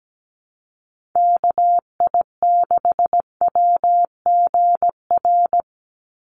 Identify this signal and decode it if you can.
Morse code, heard as a keyed tone, about 17 wpm, 700 Hz. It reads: KI6WGR
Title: USA Call Signs 17wpm